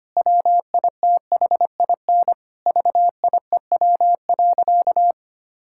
Morse code, heard as a keyed tone, about 25 wpm, 700 Hz. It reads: WITHIN VIEW.
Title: Sentences from Top 1000 Words - Review Part 6 - 25wpm